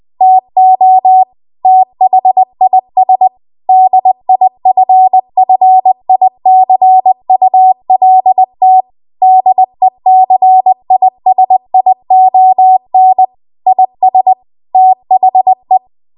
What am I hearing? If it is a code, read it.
Morse code, heard as a keyed tone, about 20 wpm, 750 Hz. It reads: TO THIS DIFFICULT DECISION IS THE